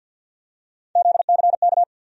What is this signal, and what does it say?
Morse code, heard as a keyed tone, about 46 wpm, 700 Hz. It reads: ZCK
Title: USA Call Sign Postfixes - 3 Letters 50wpm